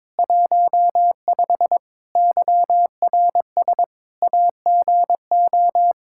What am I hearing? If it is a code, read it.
Morse code, heard as a keyed tone, about 22 wpm, 700 Hz. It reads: 15 YRS AGO
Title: K1USN SST QSO Practice 22wpm